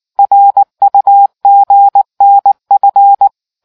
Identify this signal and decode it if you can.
Morse code, heard as a keyed tone, about 19 wpm, 800 Hz. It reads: RUGNF